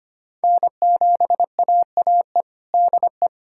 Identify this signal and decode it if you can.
Morse code, heard as a keyed tone, about 25 wpm, 700 Hz. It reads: N7AAE DE
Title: USA Callsign DE Callsign 25wpm